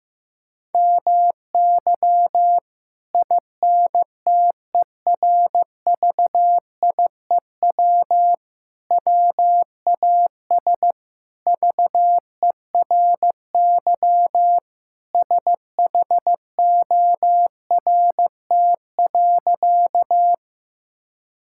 Morse code, heard as a keyed tone, about 15 wpm, 700 Hz. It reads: MY INTERVIEW WAS VERY SHORT.
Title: Sentences from Top 1600 Words 15wpm